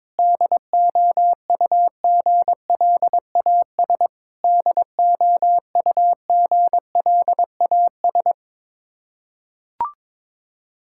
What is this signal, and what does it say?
Morse code, heard as a keyed tone, about 22 wpm, 700 Hz. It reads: DOUGLAS DOUGLAS
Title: Practice QSO - Exchange 1 - Beginning - 22wpm